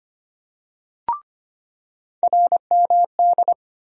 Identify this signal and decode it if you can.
Morse code, heard as a keyed tone, about 25 wpm, 700 Hz. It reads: RMD
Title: USA Call Sign Postfixes - 3 Letters 25wpm